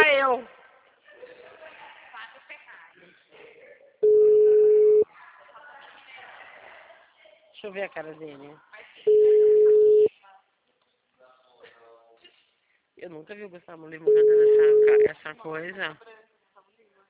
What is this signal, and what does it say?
Ringback tone (Europe, 425 Hz), with speech.
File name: out- -2001- - - .7062